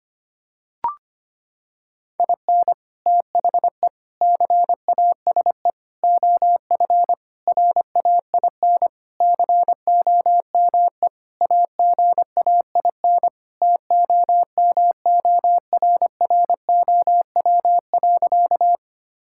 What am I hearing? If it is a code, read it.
Morse code, heard as a keyed tone, about 25 wpm, 700 Hz. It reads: IN THE CASE OF RAIN COME AGAIN TOMORROW.
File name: practice / 25wpm / Sentences with 4-Word Phrases - 25wpm